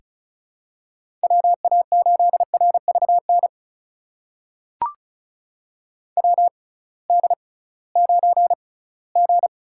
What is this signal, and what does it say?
Morse code, heard as a keyed tone, about 35 wpm, spaced as for 20 wpm, 700 Hz. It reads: WA8RVN WD9G
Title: USA Call Signs - 6x Character Spacing - 35wpm